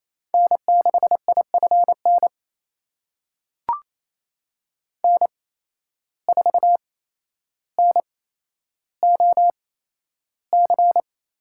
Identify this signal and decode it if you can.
Morse code, heard as a keyed tone, about 28 wpm, spaced as for 12 wpm, 700 Hz. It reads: N6IFN N4NOC
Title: USA Call Signs - 5 Characters - 8x Character Spacing - 28wpm